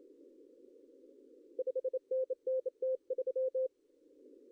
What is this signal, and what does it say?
Morse code, heard as a keyed tone, about 25 wpm, 520 Hz. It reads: 5NN T3
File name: CW / cqww-cw-k5tr